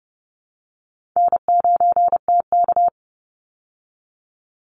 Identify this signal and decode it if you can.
Morse code, heard as a keyed tone, about 30 wpm, 700 Hz. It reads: N9TK
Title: Easy Call Signs Repeated 3x - 30wpm